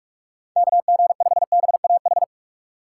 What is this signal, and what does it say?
Morse code, heard as a keyed tone, about 44 wpm, 700 Hz. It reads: KG5BAH